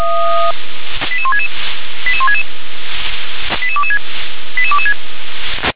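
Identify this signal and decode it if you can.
Five-tone selective call (ZVEI), heard as quick runs of five tones, tones of about 80 ms. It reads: 88166 88166 88226 88226
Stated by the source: ZVEI five-tone selective call received via SDR.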